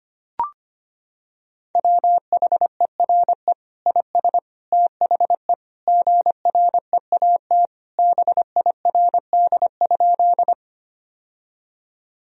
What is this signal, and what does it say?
Morse code, heard as a keyed tone, about 25 wpm, 700 Hz. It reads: WHERE IS THE GREAT BIRD?
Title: Sentences from Top 1000 Words - Review Part 2 - 25wpm